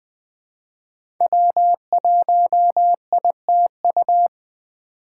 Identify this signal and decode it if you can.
Morse code, heard as a keyed tone, about 20 wpm, 700 Hz. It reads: W1ITU
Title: USA Call Signs - 5 Characters - 2x Character Spacing - 20wpm